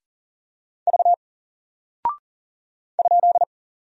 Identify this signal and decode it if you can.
Morse code, heard as a keyed tone, about 40 wpm, 700 Hz. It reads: V ?
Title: Single Character (Letter:Number:Punctuation:Prosign) Alt 40wpm